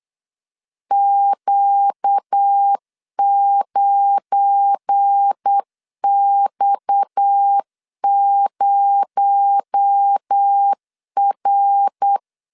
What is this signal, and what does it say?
Morse code, heard as a keyed tone, about 8 wpm, 800 Hz. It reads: Q9X0R
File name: CW / DET / 1 9 1500C S C 8wpm 5653